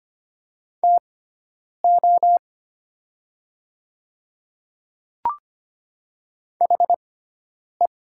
Morse code, heard as a keyed tone, about 25 wpm, spaced as for 9 wpm, 700 Hz. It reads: TO HE
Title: 6x Character Spacing - Top 100 Words - 25wpm